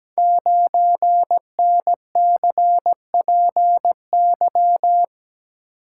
Morse code, heard as a keyed tone, about 17 wpm, 700 Hz. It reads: 9NCPY
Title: Code Groups - 5 Letter:Numbers - 17wpm